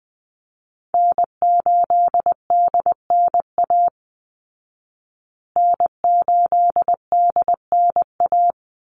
Morse code, heard as a keyed tone, about 20 wpm, 700 Hz. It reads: N8DNA N8DNA
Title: Call Signs Repeated 3x - 20wpm